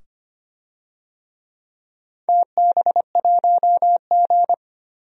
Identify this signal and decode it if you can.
Morse code, heard as a keyed tone, about 25 wpm, 700 Hz. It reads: TB1G
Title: DXCC-100 Call Signs - 4 Characters - 25wpm